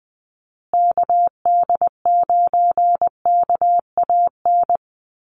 Morse code, heard as a keyed tone, about 20 wpm, 700 Hz. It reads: KD9KAN
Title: USA Call Signs Repeated 3x 20wpm